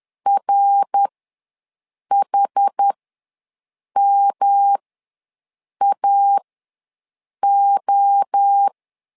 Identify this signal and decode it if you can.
Morse code, heard as a keyed tone, about 10 wpm, spaced as for 6 wpm, 800 Hz. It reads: RHMAO
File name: LEZIONI / 3 MWRO / 3 MWRO ws 10wpm 2310